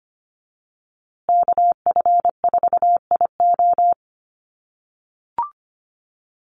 Morse code, heard as a keyed tone, about 25 wpm, 700 Hz. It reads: KF4IO
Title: USA Call Signs Repeated 2x 25wpm